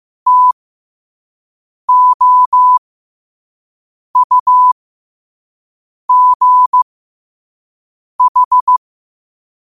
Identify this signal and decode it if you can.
Morse code, heard as a keyed tone, about 15 wpm, spaced as for 5 wpm, 1000 Hz. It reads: TOUGH
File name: qotdfiles / 5wpm